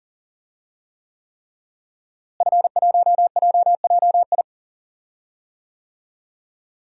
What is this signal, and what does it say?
Morse code, heard as a keyed tone, about 38 wpm, 700 Hz. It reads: F1JJI
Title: DXCC-100 Call Signs - 5 Characters - Repeated 3x - 40wpm